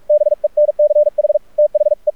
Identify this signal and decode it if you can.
Morse code, heard as a keyed tone, about 44 wpm, 600 Hz. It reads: BENCH THE
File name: archived / CRSnet